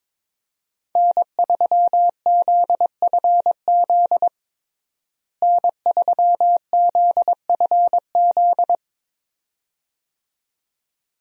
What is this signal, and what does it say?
Morse code, heard as a keyed tone, about 22 wpm, 700 Hz. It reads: N3ZFZ N3ZFZ
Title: USA Call Signs - 5 Characters - Repeated 2x - 22wpm